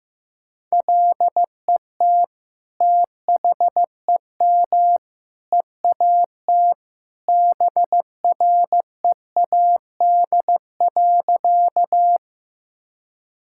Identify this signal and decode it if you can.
Morse code, heard as a keyed tone, about 15 wpm, 700 Hz. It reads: LET THEM EAT BREAD.
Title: Sentences from Top 1600 Words 15wpm